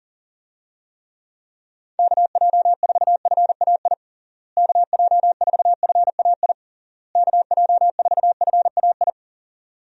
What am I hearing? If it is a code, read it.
Morse code, heard as a keyed tone, about 40 wpm, 700 Hz. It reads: KJ4FAI KJ4FAI KJ4FAI
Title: USA Call Signs - 6 Characters - Repeated 3x - 40wpm